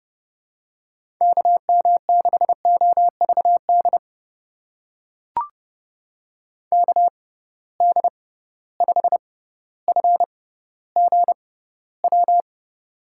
Morse code, heard as a keyed tone, about 30 wpm, spaced as for 15 wpm, 700 Hz. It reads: KM6OVD KD5FGW
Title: USA Call Signs - 6 Characters - 6x Character Spacing - 30wpm